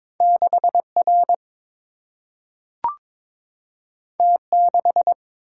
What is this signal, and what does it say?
Morse code, heard as a keyed tone, about 22 wpm, 700 Hz. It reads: 6 R T 6
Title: Single-Letter Single-Number 22wpm